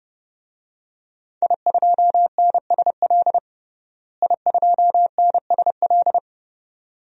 Morse code, heard as a keyed tone, about 31 wpm, 700 Hz. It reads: I2NSL I2NSL
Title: DXCC-100 Call Signs - 5 Characters - Repeated 2x - 30wpm